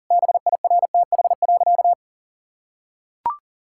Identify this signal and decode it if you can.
Morse code, heard as a keyed tone, about 40 wpm, 700 Hz. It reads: BIRTH.
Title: Sentences from Top 1000 Words - Review Part 5 - 40wpm